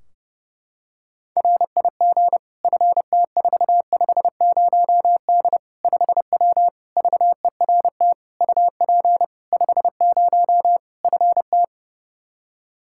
Morse code, heard as a keyed tone, about 30 wpm, 700 Hz. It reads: RIG FT450D 5W VERT UP 50 FT
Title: Practice QSO - Exchange 2 - Info - 30wpm